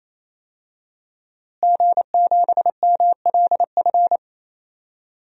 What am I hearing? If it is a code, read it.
Morse code, heard as a keyed tone, about 28 wpm, 700 Hz. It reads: G7MLF
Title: DXCC-100 Call Signs - 5 Characters - Repeated 3x - 28wpm